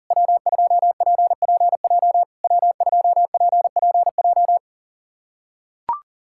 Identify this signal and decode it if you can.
Morse code, heard as a keyed tone, about 40 wpm, 700 Hz. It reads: W2PPJ W2PPJ
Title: SOTA QSO Practice 40wpm